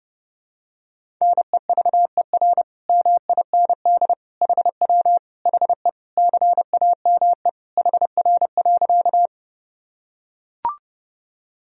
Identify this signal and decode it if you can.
Morse code, heard as a keyed tone, about 30 wpm, 700 Hz. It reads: NEVER MIND HW HE CAME HR.
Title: Sentences from Top 500 Words and Abbreviations 30wpm